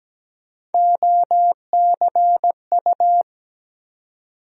Morse code, heard as a keyed tone, about 17 wpm, 700 Hz. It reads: OCU
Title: USA Call Sign Postfixes - 3 Letters 17wpm